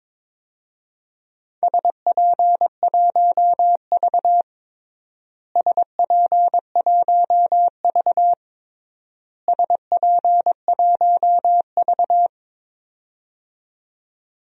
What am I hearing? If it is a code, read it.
Morse code, heard as a keyed tone, about 21 wpm, 700 Hz. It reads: SP1V SP1V SP1V
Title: DXCC-100 Call Signs - 4 Characters - Repeated 3x - 22wpm